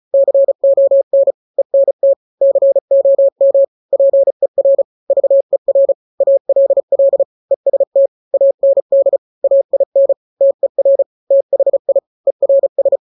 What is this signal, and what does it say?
Morse code, heard as a keyed tone, about 34 wpm, 550 Hz. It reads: CON ENT COM PER VER ALL EST AND AIN TER THI ERS
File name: 3-grams 35 30 WPM